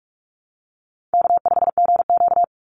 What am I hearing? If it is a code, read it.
Morse code, heard as a keyed tone, about 44 wpm, 700 Hz. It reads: K5GQ
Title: Call Signs - 45wpm